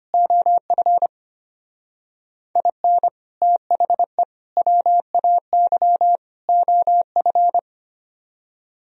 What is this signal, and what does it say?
Morse code, heard as a keyed tone, about 30 wpm, 700 Hz. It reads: OF IN THE WAY OF
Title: Speed-Racing - Sets of 4 Words 25wpm